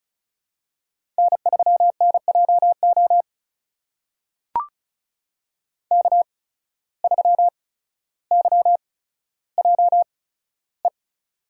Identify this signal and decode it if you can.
Morse code, heard as a keyed tone, about 36 wpm, spaced as for 15 wpm, 700 Hz. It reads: N3NJO K3YJE
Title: USA Call Signs - 5 Characters - 8x Character Spacing - 35wpm